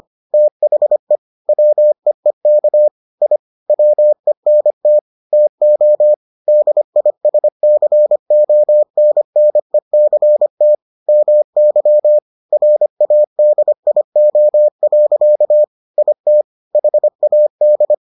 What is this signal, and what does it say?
Morse code, heard as a keyed tone, about 25 wpm, 600 Hz. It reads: THE WEEK I WENT TO DISCONNECT MY RADIO. IT HAD